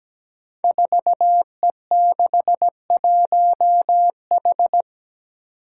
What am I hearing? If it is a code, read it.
Morse code, heard as a keyed tone, about 17 wpm, 700 Hz. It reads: 4E61H
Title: Five Challenging Letter-Number Combinations 17wpm